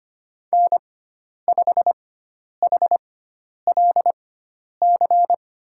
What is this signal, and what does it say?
Morse code, heard as a keyed tone, about 25 wpm, spaced as for 10 wpm, 700 Hz. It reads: N5HLC
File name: Lesson 16g - Letter c - Call Signs 25wpm